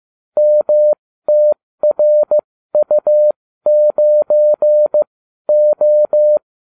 Morse code, heard as a keyed tone, about 15 wpm, spaced as for 12 wpm, 600 Hz. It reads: MTRU9O
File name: cw / ubacw29 2